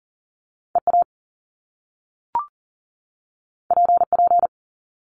Morse code, heard as a keyed tone, about 41 wpm, 700 Hz. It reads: EA PP
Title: Top Two Letter Combinations in Words 40wpm